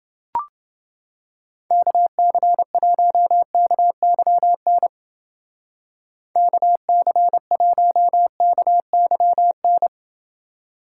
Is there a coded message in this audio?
Yes: Morse code from 1.7 to 9.9 s, heard as a keyed tone, about 29 wpm, 700 Hz. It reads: KC1KYN KC1KYN